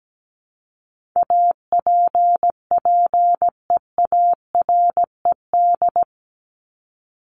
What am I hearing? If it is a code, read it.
Morse code, heard as a keyed tone, about 17 wpm, 700 Hz. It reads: APPEARED
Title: ED Suffix - Top 100 Words with Base Words 17wpm